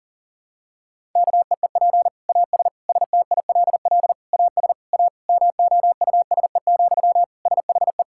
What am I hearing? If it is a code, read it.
Morse code, heard as a keyed tone, about 40 wpm, 700 Hz. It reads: KEEP AS STILL AS A MOUSE, SHE
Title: Sentences from Top 1000 Words - Review Part 5 - 40wpm